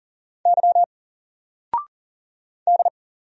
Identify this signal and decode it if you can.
Morse code, heard as a keyed tone, about 40 wpm, 700 Hz. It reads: Y D
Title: Single Character (Letter:Number:Punctuation:Prosign) 40wpm